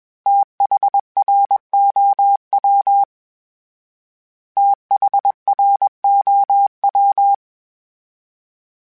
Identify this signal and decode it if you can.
Morse code, heard as a keyed tone, about 21 wpm, 800 Hz. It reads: THROW THROW
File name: TH / TH mp3s / TH Prefixes 3x2 @20